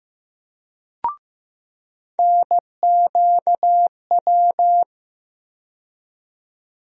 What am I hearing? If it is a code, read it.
Morse code, heard as a keyed tone, about 15 wpm, 700 Hz. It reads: NQW
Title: USA Call Sign Postfixes - 3 Letters 15wpm